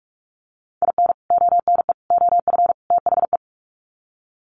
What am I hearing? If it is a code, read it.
Morse code, heard as a keyed tone, about 44 wpm, 700 Hz. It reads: IN ONE OF THE
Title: Sets of 4 Words 45wpm